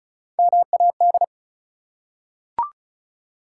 Morse code, heard as a keyed tone, about 34 wpm, 700 Hz. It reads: MAD
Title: Three Letter Words - 35wpm